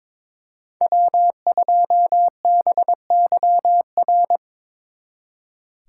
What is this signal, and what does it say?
Morse code, heard as a keyed tone, about 24 wpm, 700 Hz. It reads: W2BYR
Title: Speed Racing - USA 1x3 Call Signs 22wpm